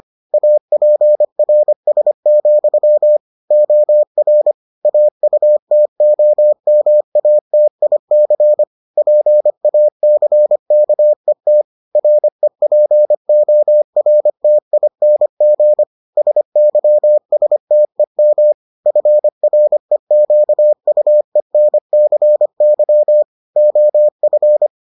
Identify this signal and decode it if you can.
Morse code, heard as a keyed tone, about 25 wpm, 600 Hz. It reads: APRS, OR AUTOMATIC PACKET REPORTING SYSTEM FREQUENCY OF